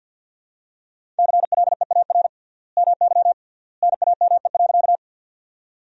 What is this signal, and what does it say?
Morse code, heard as a keyed tone, about 50 wpm, 700 Hz. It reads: CLEAR MY NAME.
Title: 3x Word-Spacing - Sentences from Top 500 Words - 50wpm